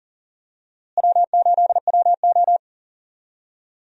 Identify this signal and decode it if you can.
Morse code, heard as a keyed tone, about 39 wpm, 700 Hz. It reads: W8WO
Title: USA Call Signs - 4 Characters - Repeated 2x - 40wpm